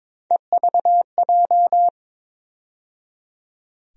Morse code, heard as a keyed tone, about 22 wpm, 700 Hz. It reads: EVJ
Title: USA Call Sign Postfixes - 3 Letters 22wpm